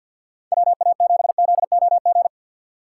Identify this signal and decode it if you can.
Morse code, heard as a keyed tone, about 50 wpm, 700 Hz. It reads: WA7ZOG
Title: USA Call Signs - 6 Characters - 3x Character Spacing - 50wpm